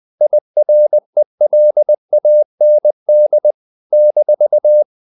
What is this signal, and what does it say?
Morse code, heard as a keyed tone, about 20 wpm, 600 Hz. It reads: IRELAND -
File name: CW / Practice / NationalCapitals20